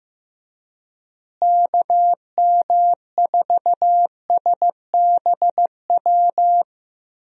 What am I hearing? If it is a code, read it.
Morse code, heard as a keyed tone, about 15 wpm, 700 Hz. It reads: KM4SBW
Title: USA Call Signs - 6 Characters - Repeated 3x - 15wpm